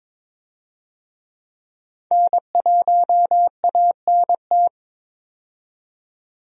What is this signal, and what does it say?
Morse code, heard as a keyed tone, about 22 wpm, 700 Hz. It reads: N1ANT
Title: USA Call Signs - 5 Characters - 22wpm